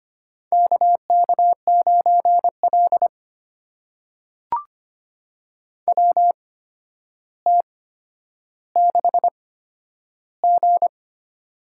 Morse code, heard as a keyed tone, about 26 wpm, spaced as for 11 wpm, 700 Hz. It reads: KK9L WT6G
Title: USA Call Signs - 4 Characters - 8x Character Spacing - 25wpm